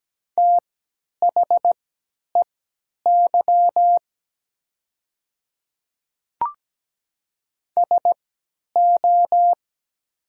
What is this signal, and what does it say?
Morse code, heard as a keyed tone, about 17 wpm, spaced as for 10 wpm, 700 Hz. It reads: THEY SO
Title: 3x Character Spacing - Top 100 Words - 17wpm